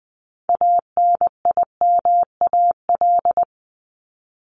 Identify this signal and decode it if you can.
Morse code, heard as a keyed tone, about 20 wpm, 700 Hz. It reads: ANIMAL
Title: ES:S Suffix - Top 100 Words with Base Words 20wpm